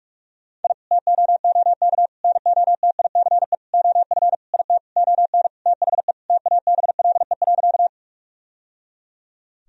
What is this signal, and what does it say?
Morse code, heard as a keyed tone, about 41 wpm, 700 Hz. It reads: I TOOK NOTICE OF IT ON THE TABLE.
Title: Sentences from Top 1000 Words - Review Part 2 - 45wpm